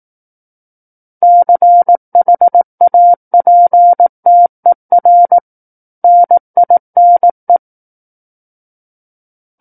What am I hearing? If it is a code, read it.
Morse code, heard as a keyed tone, about 18 wpm, 700 Hz. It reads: CHAPTER NINE
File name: 18 WPM